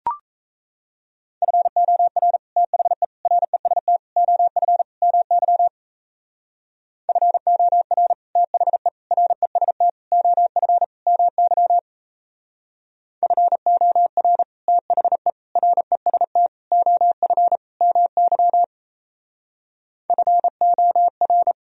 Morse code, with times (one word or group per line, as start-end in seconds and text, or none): 1.4-2.4 FOR
2.6-3.0 THE
3.3-4.0 REST
4.2-4.8 OF
5.0-5.7 MY
7.1-8.1 FOR
8.4-8.9 THE
9.1-9.9 REST
10.1-10.8 OF
11.1-11.8 MY
13.2-14.4 FOR
14.7-15.3 THE
15.6-16.5 REST
16.7-17.6 OF
17.8-18.6 MY
20.1-21.5 FOR